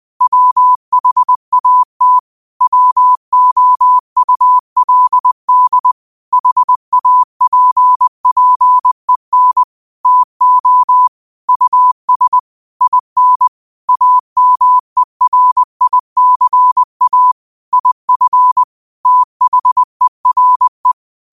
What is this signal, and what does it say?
Morse code, heard as a keyed tone, about 20 wpm, 1000 Hz. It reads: WHAT WOULD HAPPEN TO US IN AMERICA IF THERE